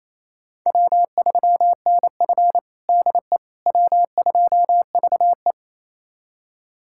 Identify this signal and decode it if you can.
Morse code, heard as a keyed tone, about 28 wpm, 700 Hz. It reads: W3NF DE W2VE
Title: USA Callsign DE Callsign - 4 Characters - 28wpm